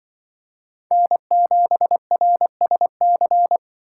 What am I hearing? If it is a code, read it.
Morse code, heard as a keyed tone, about 24 wpm, 700 Hz. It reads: N7RSC
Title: Speed Racing - USA Call Signs 20wpm